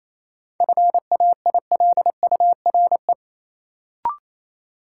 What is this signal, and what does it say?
Morse code, heard as a keyed tone, about 28 wpm, 700 Hz. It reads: FAILURE
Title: Sentences from Top 900 Words - New Word Drill - 28wpm